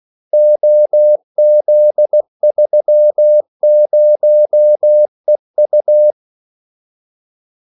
Morse code, heard as a keyed tone, about 16 wpm, 600 Hz. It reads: OZ30EU
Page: Kaldesignaler 16 wpm